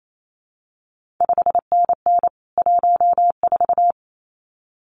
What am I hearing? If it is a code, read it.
Morse code, heard as a keyed tone, about 28 wpm, 700 Hz. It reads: 5NN 14
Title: CQ WW DX Contest Exchange - 28wpm